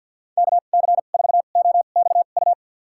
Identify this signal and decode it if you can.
Morse code, heard as a keyed tone, about 52 wpm, spaced as for 36 wpm, 700 Hz. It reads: KC4OXU
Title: USA Call Signs - 2x Character Spacing - 50wpm